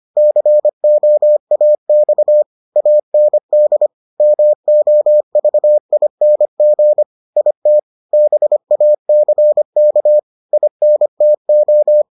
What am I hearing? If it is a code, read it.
Morse code, heard as a keyed tone, about 25 wpm, 600 Hz. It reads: COAX AND MOVING IT BACK INTO